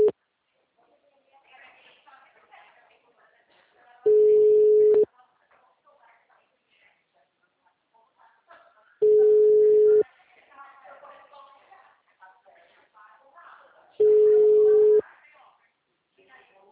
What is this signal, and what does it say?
Ringback tone (Europe, 425 Hz), with speech.
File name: out- -2019- - - .1186